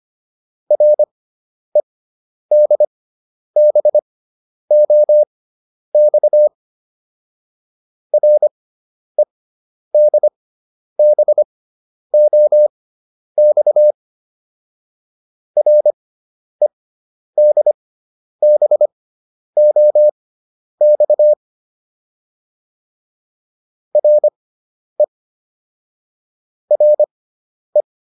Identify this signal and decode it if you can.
Morse code, heard as a keyed tone, about 25 wpm, spaced as for 10 wpm, 600 Hz. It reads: REDBOX REDBOX REDBOX RE RE
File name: RE6 10